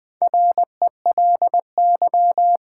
Morse code, heard as a keyed tone, about 20 wpm, 700 Hz. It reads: RELY
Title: Sentences from Top 1200 Words - New Word Drill Alt - 20wpm